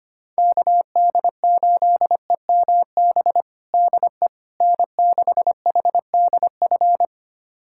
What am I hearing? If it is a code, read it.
Morse code, heard as a keyed tone, about 25 wpm, 700 Hz. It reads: KD8EMB DE N6HDF
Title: USA Callsign DE Callsign 25wpm